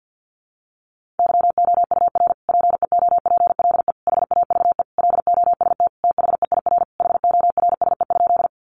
Morse code, heard as a keyed tone, about 50 wpm, 700 Hz. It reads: YOUR PEOPLE HAVE LOST THEIR HORSE?
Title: Sentences from Top 1400 Words 50wpm